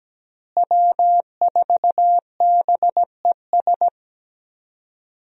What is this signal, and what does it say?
Morse code, heard as a keyed tone, about 17 wpm, 700 Hz. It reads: W4BES
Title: USA Call Signs - 5 Characters - Repeated 2x - 17wpm